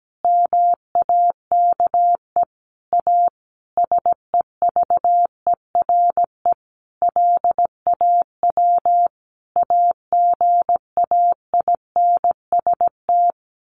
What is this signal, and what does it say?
Morse code, heard as a keyed tone, about 17 wpm, 700 Hz. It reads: MAKE A SEVERE LAW AGAINST
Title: Sentences from Top 1400 Words 17wpm